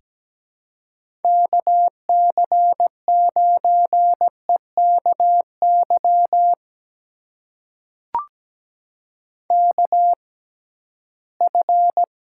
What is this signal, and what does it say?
Morse code, heard as a keyed tone, about 17 wpm, spaced as for 13 wpm, 700 Hz. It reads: KC9EKY KF
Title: USA Call Signs - 6 Characters - 6x Character Spacing - 17wpm